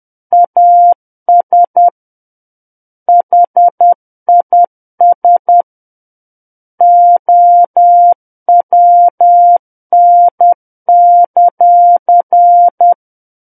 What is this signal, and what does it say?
Morse code, heard as a keyed tone, about 10 wpm, 700 Hz. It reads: AS HIS OWN;
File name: Book One / 01-Chapter 1 - 10 WPM